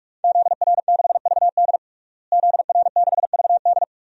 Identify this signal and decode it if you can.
Morse code, heard as a keyed tone, about 44 wpm, 700 Hz. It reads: ZR6VD ZR6VD
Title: DXCC-100 Call Signs - 5 Characters - Repeated 2x - 45wpm